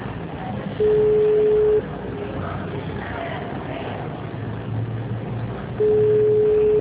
Ringback tone (Europe, 425 Hz).